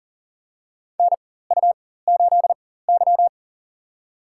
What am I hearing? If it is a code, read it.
Morse code, heard as a keyed tone, about 40 wpm, spaced as for 19 wpm, 700 Hz. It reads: NU8Y
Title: USA Call Signs - 4 Characters - 4x Character Spacing - 40wpm